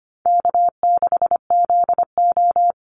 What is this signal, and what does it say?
Morse code, heard as a keyed tone, about 25 wpm, 700 Hz. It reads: K6ZO
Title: Easy Call Signs - 25wpm